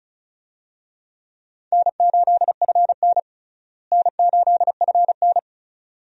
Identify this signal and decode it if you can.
Morse code, heard as a keyed tone, about 36 wpm, 700 Hz. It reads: N8FN N8FN
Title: USA Call Signs - 4 Characters - Repeated 2x - 35wpm